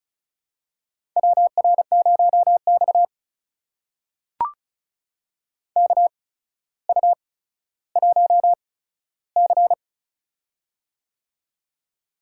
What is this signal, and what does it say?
Morse code, heard as a keyed tone, about 36 wpm, spaced as for 15 wpm, 700 Hz. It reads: WR0X KU1C
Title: USA Call Signs - 4 Characters - 8x Character Spacing - 35wpm